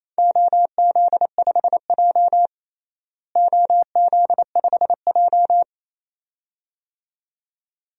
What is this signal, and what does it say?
Morse code, heard as a keyed tone, about 28 wpm, 700 Hz. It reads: OZ5J OZ5J
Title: DXCC-100 Call Signs - 4 Characters - Repeated 2x - 28wpm